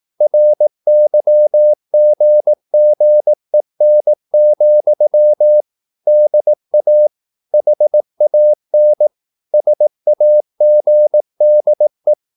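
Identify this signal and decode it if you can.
Morse code, heard as a keyed tone, about 18 wpm, 600 Hz. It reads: RYGGEN, DA HAN SAGDE
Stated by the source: Slow (QRS) Morse practice at 18 wpm, CW med Gnister.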